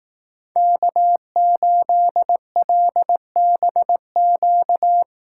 Morse code, heard as a keyed tone, about 18 wpm, 700 Hz. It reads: K8LBQ
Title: Speed Racing - USA 1x3 Call Signs 15wpm